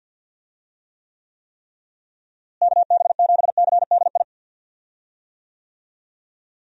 Morse code, heard as a keyed tone, about 48 wpm, 700 Hz. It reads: KB7CDI
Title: USA Call Signs - 6 Characters - Repeated 3x - 50wpm